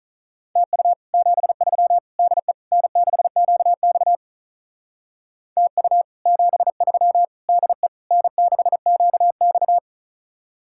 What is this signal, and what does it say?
Morse code, heard as a keyed tone, about 39 wpm, 700 Hz. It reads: TU 73 DE N6QX TU 73 DE N6QX
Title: Speed-Racing Common QSO Phrases Encore - 35wpm